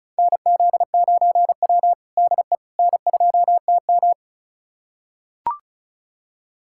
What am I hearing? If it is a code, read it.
Morse code, heard as a keyed tone, about 36 wpm, 700 Hz. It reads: NZ9W DE N2TM
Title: USA Callsign DE Callsign - 4 Characters - 35wpm